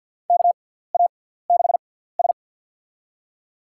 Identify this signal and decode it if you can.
Morse code, heard as a keyed tone, about 52 wpm, spaced as for 17 wpm, 700 Hz. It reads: KA6S
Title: USA Call Signs - 4 Characters - 6x Character Spacing - 50wpm